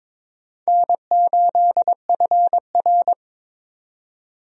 Morse code, heard as a keyed tone, about 22 wpm, 700 Hz. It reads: N8FR
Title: USA Call Signs - 4 Characters - 22wpm